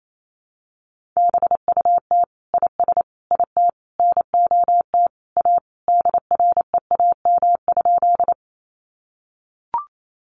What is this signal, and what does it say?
Morse code, heard as a keyed tone, about 28 wpm, 700 Hz. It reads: BUT IS IT NOT A DREAM?
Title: Sentences from Top 1500 Words 28wpm